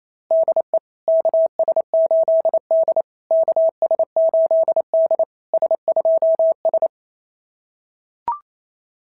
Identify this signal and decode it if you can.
Morse code, heard as a keyed tone, about 27 wpm, 650 Hz. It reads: DE KS8D KS8D S2S
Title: SOTA QSO Practice 28wpm